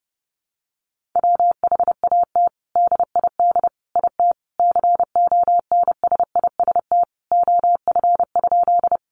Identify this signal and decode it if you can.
Morse code, heard as a keyed tone, about 30 wpm, 700 Hz. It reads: WHAT DID IT CONSIST OF?
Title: Sentences from Top 1400 Words 30wpm